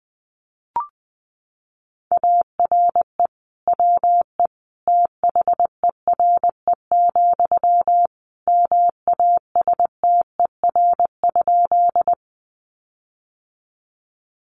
Morse code, heard as a keyed tone, about 19 wpm, 700 Hz. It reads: ARE WE THERE, MASTER?
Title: Sentences from Top 1300 Words 20wpm